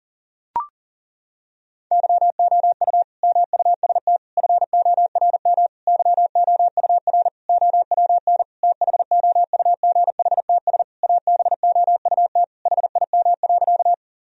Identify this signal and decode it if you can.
Morse code, heard as a keyed tone, about 37 wpm, 700 Hz. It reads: YOU MUST FORM YOUR OWN THOUGHTS ABOUT HIM.
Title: Sentences from Top 1000 Words - Review Part 1 - 40wpm